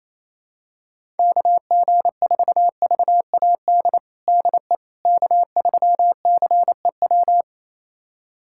Morse code, heard as a keyed tone, about 28 wpm, 700 Hz. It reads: KG4VAD DE K3CEW
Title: USA Callsign DE Callsign 28wpm